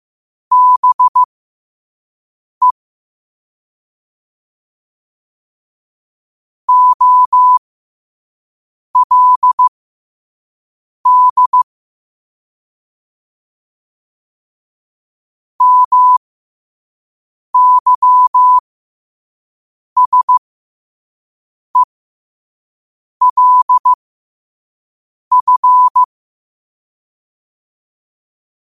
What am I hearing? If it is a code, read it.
Morse code, heard as a keyed tone, about 15 wpm, spaced as for 5 wpm, 1000 Hz. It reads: BE OLD MYSELF
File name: qotdfiles / 5wpm